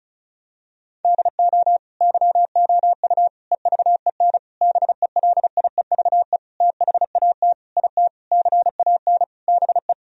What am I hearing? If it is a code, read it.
Morse code, heard as a keyed tone, about 36 wpm, 700 Hz. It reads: DO YOU EVEN BELIEVE THAT IT CAN BE
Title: Sentences from Top 1000 Words - Review Part 1 - 35wpm